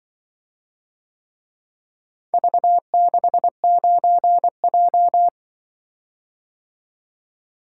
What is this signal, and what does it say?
Morse code, heard as a keyed tone, about 24 wpm, 700 Hz. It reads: V69J
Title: Speed-Racing - Four Letter-Number Combinations - 20wpm